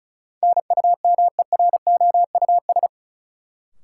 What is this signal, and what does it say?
Morse code, heard as a keyed tone, about 36 wpm, 700 Hz. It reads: NUMEROUS
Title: Sentences from Top 1400 Words - New Word Drill - 35wpm